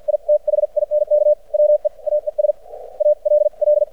Morse code, heard as a keyed tone, about 52 wpm, spaced as for 43 wpm, 600 Hz. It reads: I THINK WE AES ALL
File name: archived / CRSnet